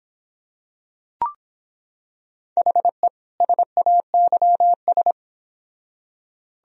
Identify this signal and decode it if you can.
Morse code, heard as a keyed tone, about 26 wpm, 700 Hz. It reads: HE SAYS